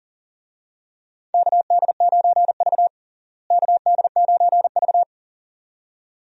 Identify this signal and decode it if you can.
Morse code, heard as a keyed tone, about 39 wpm, 700 Hz. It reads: KD9V KD9V
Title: USA Call Signs - 4 Characters - Repeated 2x - 40wpm